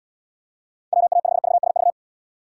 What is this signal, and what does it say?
Morse code, heard as a keyed tone, about 78 wpm, 700 Hz. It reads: WA5VSV